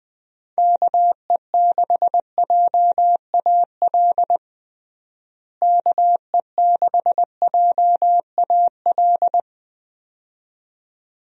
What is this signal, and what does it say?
Morse code, heard as a keyed tone, about 20 wpm, 700 Hz. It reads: KE6JAL KE6JAL
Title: USA Call Signs - 6 Characters - Repeated 2x - 20wpm